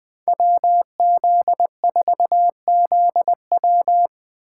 Morse code, heard as a keyed tone, about 21 wpm, 700 Hz. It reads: WZ4ZW
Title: USA Call Signs - 5 Characters - 4x Character Spacing - 20wpm